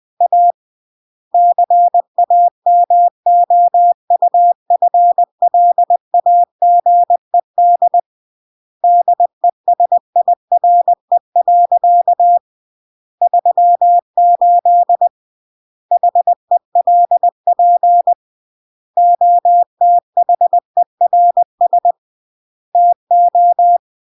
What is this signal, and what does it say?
Morse code, heard as a keyed tone, about 20 wpm, 700 Hz. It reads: A CAMOUFLAGED DESIRE. 38 HELP OTHERS TO
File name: CW / jodorowsky / 82 commandments 20-20